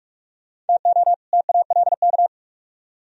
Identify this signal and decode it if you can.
Morse code, heard as a keyed tone, about 56 wpm, 700 Hz. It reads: TO TALK
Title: Speed-Racing - Sets of 2 Words 45wpm